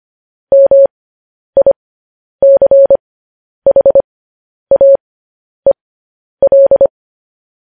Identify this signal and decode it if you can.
Morse code, heard as a keyed tone, about 25 wpm, spaced as for 10 wpm, 550 Hz. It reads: MICHAEL
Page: news headlines in morse code at 10 wpm